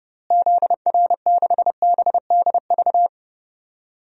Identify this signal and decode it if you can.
Morse code, heard as a keyed tone, about 30 wpm, 700 Hz. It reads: ZR6BDV
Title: DXCC-100 Call Signs - 6 Characters - Repeated 2x - 30wpm